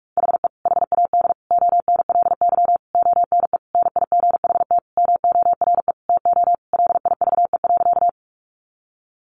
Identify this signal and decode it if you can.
Morse code, heard as a keyed tone, about 44 wpm, 700 Hz. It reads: HE HAD ONLY ONE NIGHT MORE TO LIVE.
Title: Sentences from Top 300 Words 45wpm